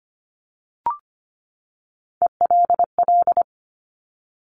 Morse code, heard as a keyed tone, about 25 wpm, 700 Hz. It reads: ELL
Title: Top Three Letter Combinations in Words 25wpm